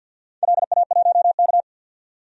Morse code, heard as a keyed tone, about 53 wpm, 700 Hz. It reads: LA1K
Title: DXCC-100 Call Signs - 4 Characters - 50wpm